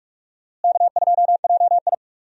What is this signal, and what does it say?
Morse code, heard as a keyed tone, about 46 wpm, 700 Hz. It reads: K2JI
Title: USA Call Signs - 4 Characters - Repeated 3x - 45wpm